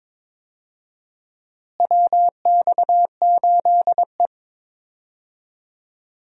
Morse code, heard as a keyed tone, about 22 wpm, 700 Hz. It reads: WX8E
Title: USA Call Signs - 4 Characters - Repeated 2x - 22wpm